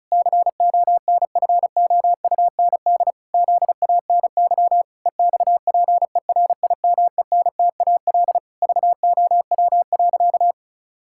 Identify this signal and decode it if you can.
Morse code, heard as a keyed tone, about 34 wpm, 700 Hz. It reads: CONFOUND ZANY EXPERIMENTAL VOW.
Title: Pangram Challenge v2 - 35wpm